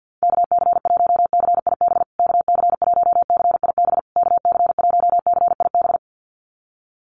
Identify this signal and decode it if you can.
Morse code, heard as a keyed tone, about 50 wpm, 700 Hz. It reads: KC1CIB KC1CIB KC1CIB
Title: USA Call Signs Repeated 3x 50wpm